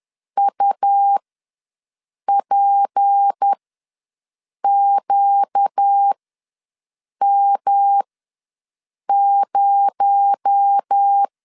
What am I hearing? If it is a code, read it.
Morse code, heard as a keyed tone, about 10 wpm, spaced as for 6 wpm, 800 Hz. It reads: UPQM0
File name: LEZIONI / 6 PKX0 / 6 PKX0 ws 10wpm 4220